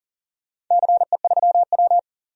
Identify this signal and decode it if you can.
Morse code, heard as a keyed tone, about 40 wpm, 700 Hz. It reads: CE3W
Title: DXCC-100 Call Signs - 4 Characters - 40wpm